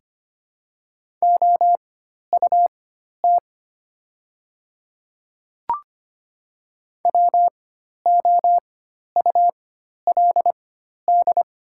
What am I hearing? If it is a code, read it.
Morse code, heard as a keyed tone, about 25 wpm, spaced as for 12 wpm, 700 Hz. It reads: OUT WOULD
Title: 4x Character Spacing - Top 100 Words - 25wpm